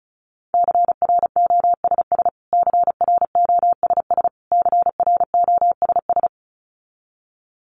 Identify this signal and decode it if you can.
Morse code, heard as a keyed tone, about 34 wpm, 700 Hz. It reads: CROSS CROSS CROSS
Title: Top 500 Words Repeated 35wpm